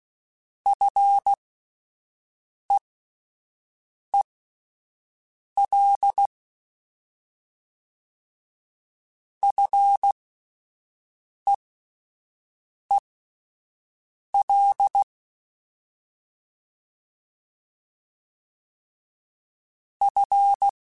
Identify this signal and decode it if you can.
Morse code, heard as a keyed tone, about 16 wpm, spaced as for 5 wpm, 800 Hz. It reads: FEEL FEEL F
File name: Fs / Single F mp3s / Single F Prefixes 3x2 Twice @5